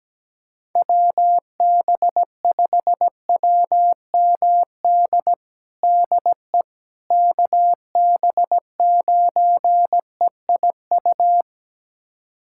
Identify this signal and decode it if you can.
Morse code, heard as a keyed tone, about 17 wpm, 700 Hz. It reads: WB5WMD DE KB9EIU
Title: USA Callsign DE Callsign - 6 Characters - 17wpm